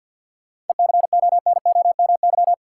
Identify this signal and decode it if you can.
Morse code, heard as a keyed tone, about 53 wpm, 700 Hz. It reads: ECONOMY
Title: Sentences from Top 1000 Words - Review Part 3 - New Word Drill - 50wpm